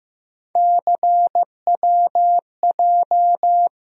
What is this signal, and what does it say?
Morse code, heard as a keyed tone, about 15 wpm, 700 Hz. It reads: CWJ
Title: USA Call Sign Postfixes - 3 Letters 15wpm